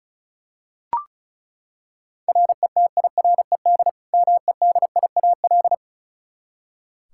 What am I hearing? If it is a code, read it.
Morse code, heard as a keyed tone, about 36 wpm, 700 Hz. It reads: RETIRED MEDIAL
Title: Practice QSO - Exchange 3 - More Info - 35wpm